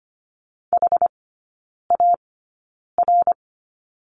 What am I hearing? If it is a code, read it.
Morse code, heard as a keyed tone, about 25 wpm, spaced as for 9 wpm, 690 Hz. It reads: HAR
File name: Lesson 14e - Number 2 - Three Characters 25wpm